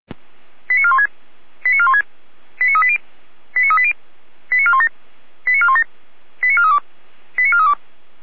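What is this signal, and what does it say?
Five-tone selective call (ZVEI), heard as quick runs of five tones, tones of about 70 ms. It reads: 89416 89416 89380 89380 89417 89417 89432 89432
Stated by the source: ZVEI five-tone call received on an SDR.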